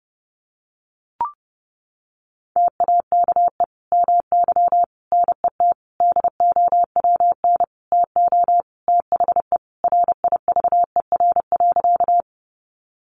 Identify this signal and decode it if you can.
Morse code, heard as a keyed tone, about 30 wpm, 700 Hz. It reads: TAKE MY NET DOWN TO THE RIVER.
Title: Sentences from Top 1400 Words 30wpm